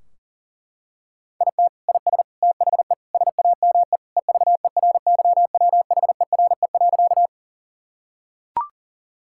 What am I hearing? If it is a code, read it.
Morse code, heard as a keyed tone, about 40 wpm, 700 Hz. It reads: IT IS THE SAME EVERYWHERE.
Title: Sentences from Top 1600 Words 40wpm